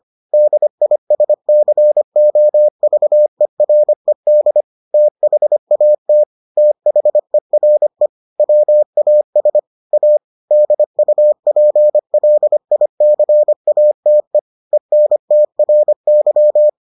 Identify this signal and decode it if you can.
Morse code, heard as a keyed tone, about 25 wpm, 600 Hz. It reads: DISCOVERED THAT THERE WAS A DUPLICATE ENTRY